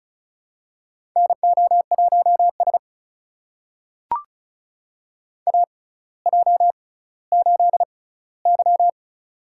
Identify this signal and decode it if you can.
Morse code, heard as a keyed tone, about 34 wpm, spaced as for 18 wpm, 700 Hz. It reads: NO1S AJ8Y
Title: USA Call Signs - 4 Characters - 6x Character Spacing - 35wpm